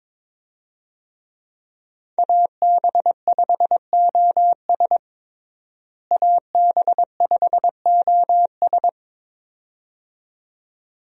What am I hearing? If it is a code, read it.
Morse code, heard as a keyed tone, about 23 wpm, 700 Hz. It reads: AB5OS AB5OS
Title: USA Call Signs - 5 Characters - Repeated 2x - 22wpm